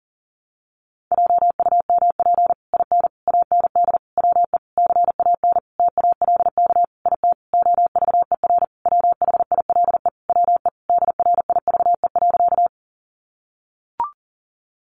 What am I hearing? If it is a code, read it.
Morse code, heard as a keyed tone, about 40 wpm, 700 Hz. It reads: JUMP IN AND WE CAN TALK IT OVER WHILE WE DRIVE.
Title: Sentences from Top 1300 Words 40wpm